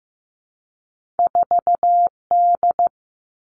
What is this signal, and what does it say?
Morse code, heard as a keyed tone, about 15 wpm, 700 Hz. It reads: 4D
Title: Two Challenging Letter-Number Combinations 15wpm